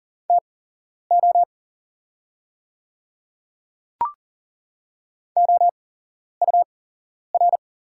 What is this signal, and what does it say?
Morse code, heard as a keyed tone, about 40 wpm, spaced as for 11 wpm, 690 Hz. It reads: TO OUR
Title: 8x Character Spacing - Top 200 Words - 40wpm